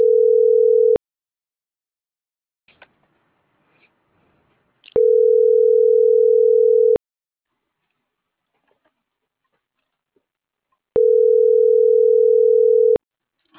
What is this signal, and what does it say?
Ringback tone (North America), 440 + 480 Hz.